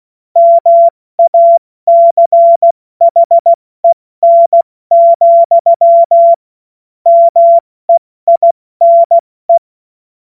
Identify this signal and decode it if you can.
Morse code, heard as a keyed tone, about 16 wpm, 680 Hz. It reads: MACHEN, MEINE
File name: Morsekurs / Witze / Witz 04. April 70 F80